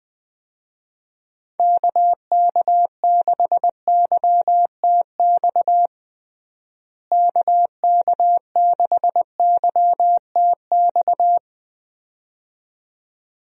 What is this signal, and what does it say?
Morse code, heard as a keyed tone, about 20 wpm, 700 Hz. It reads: KK6YTX KK6YTX
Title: USA Call Signs - 6 Characters - Repeated 2x - 20wpm